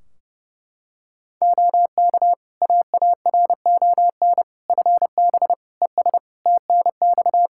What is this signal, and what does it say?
Morse code, heard as a keyed tone, about 30 wpm, 700 Hz. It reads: OK AARON FB ES TNX
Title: K1USN SST QSO Practice 30wpm